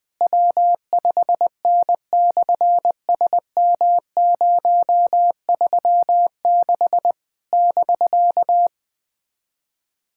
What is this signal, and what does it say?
Morse code, heard as a keyed tone, about 20 wpm, 700 Hz. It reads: W5N/SM036 <BK>
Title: SOTA QSO Practice 20wpm